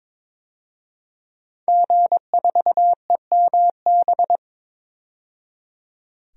Morse code, heard as a keyed tone, about 23 wpm, 700 Hz. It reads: G4EMB
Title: DXCC-100 Call Signs - 5 Characters - 22wpm